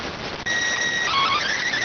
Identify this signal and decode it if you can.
SSTV (slow-scan television), PD 290.